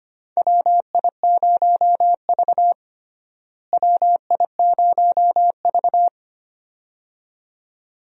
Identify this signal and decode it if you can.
Morse code, heard as a keyed tone, about 25 wpm, 700 Hz. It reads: WI0V WI0V
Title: USA Call Signs - 4 Characters - Repeated 2x - 25wpm